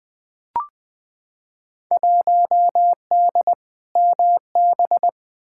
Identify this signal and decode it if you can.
Morse code, heard as a keyed tone, about 20 wpm, 700 Hz. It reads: 1D MB
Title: Field Day Exchanges Encore - 20wpm